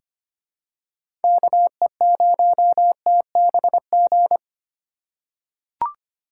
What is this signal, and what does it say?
Morse code, heard as a keyed tone, about 25 wpm, 700 Hz. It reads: KE0TBG
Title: USA Call Signs - 6 Characters - 3x Character Spacing - 25wpm